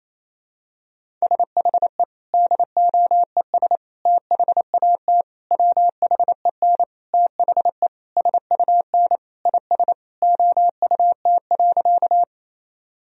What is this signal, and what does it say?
Morse code, heard as a keyed tone, about 28 wpm, 700 Hz. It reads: SHE DOES THAT WHEN THE SUN IS OUT.
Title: Sentences from Top 1000 Words - Review Part 1 - 28wpm